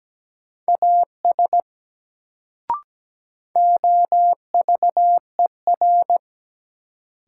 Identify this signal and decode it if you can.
Morse code, heard as a keyed tone, about 17 wpm, 700 Hz. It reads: AS OVER
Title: Sentences from Top 100 Words - New Word Drill - 17wpm